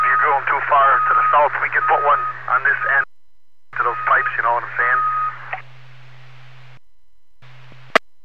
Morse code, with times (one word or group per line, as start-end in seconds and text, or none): none